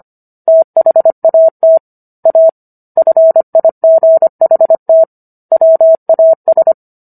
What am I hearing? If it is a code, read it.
Morse code, heard as a keyed tone, about 25 wpm, 650 Hz. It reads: THAT A FIGHT WAS